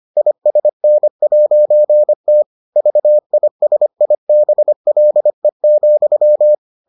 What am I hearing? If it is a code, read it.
Morse code, heard as a keyed tone, about 25 wpm, 600 Hz. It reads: ISN'T VISIBLE,